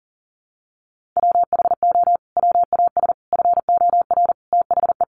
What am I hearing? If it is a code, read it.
Morse code, heard as a keyed tone, about 40 wpm, 700 Hz. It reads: WHO WAS FOR THE
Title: Sentences from Top 1400 Words 40wpm